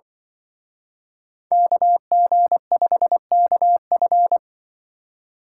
Morse code, heard as a keyed tone, about 24 wpm, 700 Hz. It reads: KG5KF
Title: Speed Racing - USA 2x2 Call Signs 20wpm